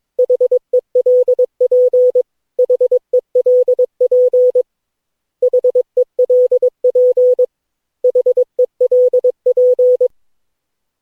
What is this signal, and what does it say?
Morse code, heard as a keyed tone, about 22 wpm, 500 Hz. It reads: HELP HELP HELP HELP
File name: cw / Session9 cw training